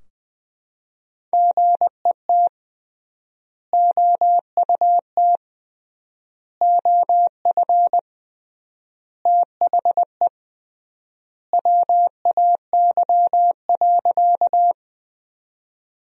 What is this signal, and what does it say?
Morse code, heard as a keyed tone, about 21 wpm, 700 Hz. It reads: GET OUT OF THE WAY.
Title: 3x Word-Spacing - Sentences from Top 200 Words - 20wpm